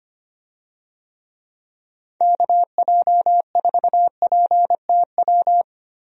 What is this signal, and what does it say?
Morse code, heard as a keyed tone, about 25 wpm, 700 Hz. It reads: KJ4PTW
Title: USA Call Signs - 6 Characters - 25wpm